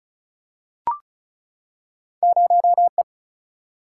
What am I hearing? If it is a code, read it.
Morse code, heard as a keyed tone, about 36 wpm, 700 Hz. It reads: 0 E
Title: Single-Letter Single-Number 35wpm